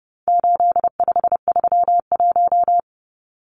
Morse code, heard as a keyed tone, about 30 wpm, 700 Hz. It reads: 8531
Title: Numbers - 4-Digit 30wpm